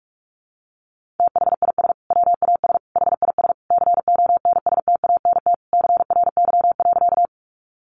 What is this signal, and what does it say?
Morse code, heard as a keyed tone, about 46 wpm, 700 Hz. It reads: THIS WAS HIS CONSTANT CRY.
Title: Sentences from Top 1500 Words 45wpm